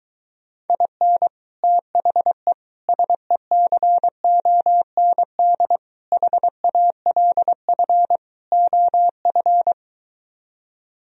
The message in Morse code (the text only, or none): IN THE SECOND HALF OF